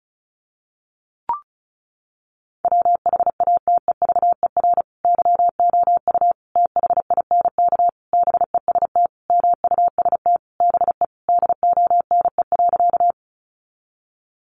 Morse code, heard as a keyed tone, about 34 wpm, 700 Hz. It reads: WHATEVER YOU THINK BEST MUST BE DONE.
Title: Sentences from Top 1100 Words 35wpm